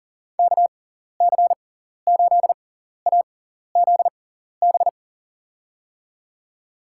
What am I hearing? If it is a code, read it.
Morse code, heard as a keyed tone, about 37 wpm, spaced as for 14 wpm, 700 Hz. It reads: KC8AZB
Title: USA Call Signs - 6 Characters - 6x Character Spacing - 40wpm